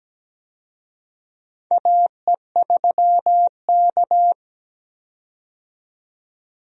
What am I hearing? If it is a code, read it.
Morse code, heard as a keyed tone, about 17 wpm, 700 Hz. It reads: AE3K
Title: USA Call Signs - 4 Characters - 17wpm